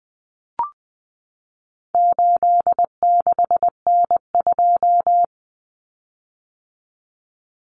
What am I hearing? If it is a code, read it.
Morse code, heard as a keyed tone, about 20 wpm, 700 Hz. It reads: 86N2
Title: Common Cut Numbers - 4-Digit 20wpm